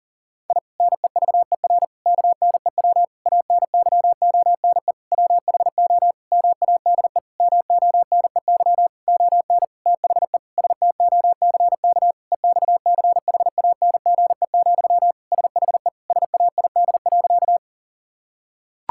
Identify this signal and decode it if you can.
Morse code, heard as a keyed tone, about 40 wpm, 700 Hz. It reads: I NEVER KNEW ANYONE WHO MADE MONEY ON THE STOCK EXCHANGE, SHE SAID.
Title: Sentences from Top 1000 Words - Review Part 6 - 40wpm